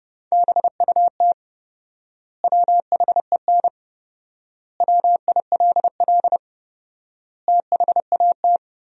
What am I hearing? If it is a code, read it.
Morse code, heard as a keyed tone, about 32 wpm, 700 Hz. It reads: BUT WHEN WILL THAT
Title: 4x Word-Spacing - Sentences from Top 100 Words - 30wpm